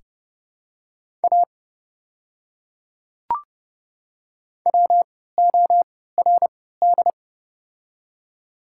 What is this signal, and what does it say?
Morse code, heard as a keyed tone, about 32 wpm, spaced as for 17 wpm, 700 Hz. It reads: A WORD
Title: 3x Character Spacing - Top 300 Words - 30wpm